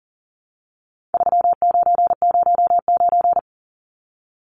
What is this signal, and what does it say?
Morse code, heard as a keyed tone, about 40 wpm, 700 Hz. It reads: 3909
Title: Numbers - 4-Digit 40wpm